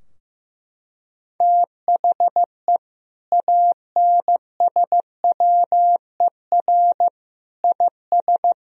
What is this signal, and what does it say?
Morse code, heard as a keyed tone, about 15 wpm, 700 Hz. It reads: THE ANSWER IS
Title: Sentences from Top 1000 Words - Review Part 2 - 15wpm